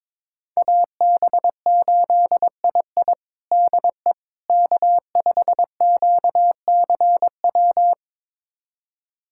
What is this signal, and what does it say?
Morse code, heard as a keyed tone, about 22 wpm, 700 Hz. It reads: AB8II DE K5QCW
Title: USA Callsign DE Callsign - 5 Characters - 22wpm